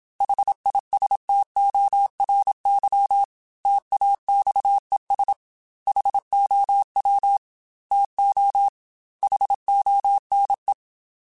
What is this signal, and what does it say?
Morse code, heard as a keyed tone, about 27 wpm, 800 Hz. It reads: HISTORY TAXES HOW TO HONE
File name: T then H Runs / T then H Runs.mp3s / T then H Runs @25